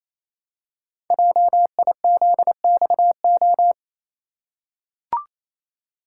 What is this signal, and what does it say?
Morse code, heard as a keyed tone, about 28 wpm, 700 Hz. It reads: JIZXO
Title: Code Groups - 5 Letters - 28wpm